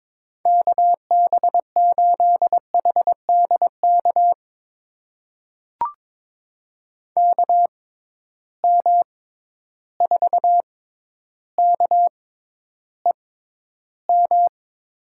Morse code, heard as a keyed tone, about 22 wpm, spaced as for 11 wpm, 700 Hz. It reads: KB8HDK KM4KEM
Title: USA Call Signs - 6 Characters - 6x Character Spacing - 22wpm